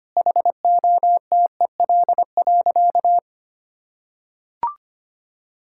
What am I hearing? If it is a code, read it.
Morse code, heard as a keyed tone, about 25 wpm, 700 Hz. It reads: HOTEL.
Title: Sentences from Top 1600 Words 25wpm